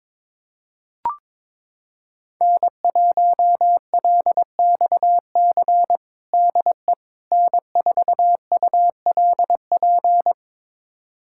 Morse code, heard as a keyed tone, about 22 wpm, 700 Hz. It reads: N1LXC DE N4ULP
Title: USA Callsign DE Callsign - 5 Characters - 22wpm